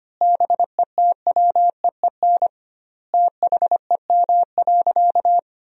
Morse code, heard as a keyed tone, about 25 wpm, 700 Hz. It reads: BETWEEN THEM.
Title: 2x Word-Spacing - Sentences from Top 400 Words - 25wpm